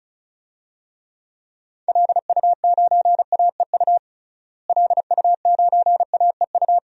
Morse code, heard as a keyed tone, about 36 wpm, 700 Hz. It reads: LU9AEU LU9AEU
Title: DXCC-100 Call Signs - 6 Characters - Repeated 2x - 35wpm